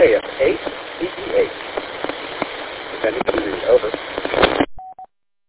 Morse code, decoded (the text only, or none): none